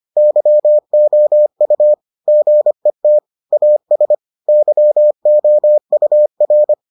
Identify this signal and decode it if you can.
Morse code, heard as a keyed tone, about 25 wpm, 600 Hz. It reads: YOU GET AS YOUR